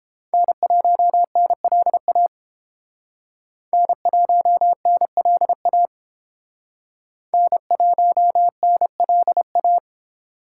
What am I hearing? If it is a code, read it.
Morse code, heard as a keyed tone, about 29 wpm, 700 Hz. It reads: N1NLA N1NLA N1NLA